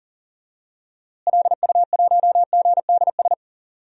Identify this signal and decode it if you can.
Morse code, heard as a keyed tone, about 40 wpm, 700 Hz. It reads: LU1GDS
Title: DXCC-100 Call Signs - 6 Characters - Repeated 3x - 40wpm